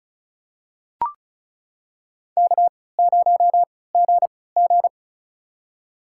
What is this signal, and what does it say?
Morse code, heard as a keyed tone, about 34 wpm, spaced as for 20 wpm, 700 Hz. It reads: K0GG
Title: USA Call Signs - 4 Characters - 3x Character Spacing - 35wpm